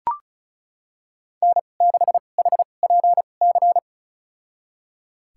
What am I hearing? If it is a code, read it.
Morse code, heard as a keyed tone, about 34 wpm, spaced as for 25 wpm, 700 Hz. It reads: N6HPC
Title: USA Call Signs - 5 Characters - 2x Character Spacing - 35wpm